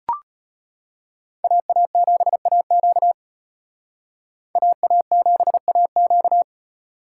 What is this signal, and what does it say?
Morse code, heard as a keyed tone, about 32 wpm, 700 Hz. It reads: AA7AQ AA7AQ